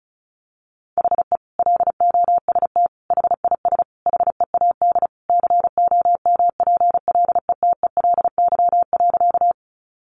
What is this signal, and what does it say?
Morse code, heard as a keyed tone, about 34 wpm, 700 Hz. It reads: HE LOST HIS HEAD COMPLETELY.
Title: Sentences from Top 1400 Words 35wpm